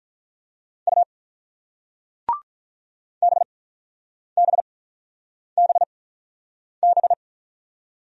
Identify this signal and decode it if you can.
Morse code, heard as a keyed tone, about 42 wpm, spaced as for 9 wpm, 690 Hz. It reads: U BBBB
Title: Speed-Racing Single Letters 35wpm